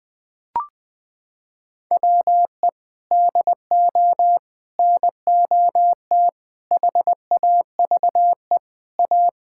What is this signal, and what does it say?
Morse code, heard as a keyed tone, about 19 wpm, 700 Hz. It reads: WE DO NOT HAVE A
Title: Sentences with 3-Word Phrases - 20wpm